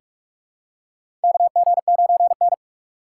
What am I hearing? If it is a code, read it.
Morse code, heard as a keyed tone, about 46 wpm, 700 Hz. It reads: KG9N
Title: USA Call Signs - 4 Characters - 4x Character Spacing - 45wpm